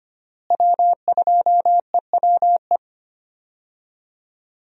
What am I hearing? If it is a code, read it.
Morse code, heard as a keyed tone, about 25 wpm, 700 Hz. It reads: W2EWE
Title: USA Call Signs - 5 Characters - Repeated 3x - 25wpm